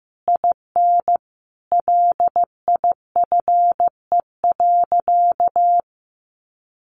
Morse code, heard as a keyed tone, about 15 wpm, 700 Hz. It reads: IN LIFE.
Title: Sentences from Top 1200 Words 15wpm